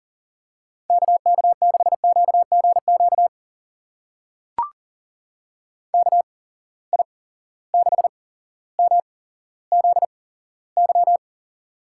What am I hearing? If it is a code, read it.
Morse code, heard as a keyed tone, about 39 wpm, spaced as for 17 wpm, 700 Hz. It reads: KK6QGQ KI6MZY
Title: USA Call Signs - 6 Characters - 8x Character Spacing - 40wpm